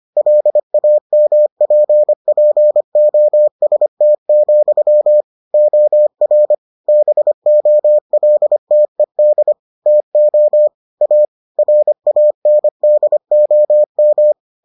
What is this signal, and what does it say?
Morse code, heard as a keyed tone, about 25 wpm, 600 Hz. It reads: LAMPPOST, OR BOLTED TO A RANDOM